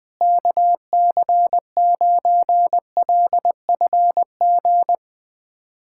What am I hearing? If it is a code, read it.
Morse code, heard as a keyed tone, about 20 wpm, 700 Hz. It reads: KC9LFG
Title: USA Callsign DE Callsign - 6 Characters - 20wpm